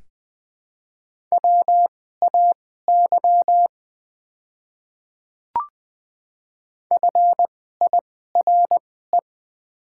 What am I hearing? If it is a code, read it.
Morse code, heard as a keyed tone, about 20 wpm, spaced as for 14 wpm, 700 Hz. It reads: WAY FIRE
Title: 2x Character Spacing - Top 500 Words - 20wpm